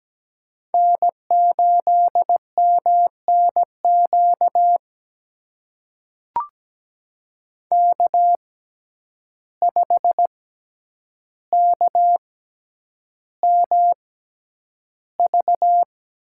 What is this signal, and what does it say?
Morse code, heard as a keyed tone, about 17 wpm, spaced as for 9 wpm, 700 Hz. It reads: N8MNQ K5KMV
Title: USA Call Signs - 5 Characters - 6x Character Spacing - 17wpm